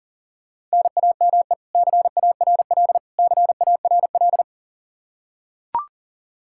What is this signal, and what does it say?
Morse code, heard as a keyed tone, about 40 wpm, 700 Hz. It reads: NAME CARL CARL
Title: K1USN SST QSO Practice 40wpm